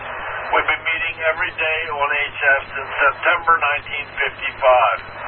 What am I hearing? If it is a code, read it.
Emergency Alert System (SAME) tones, heard as harsh buzzing bursts, with speech. It is heard from 0.6 to 4.4 s.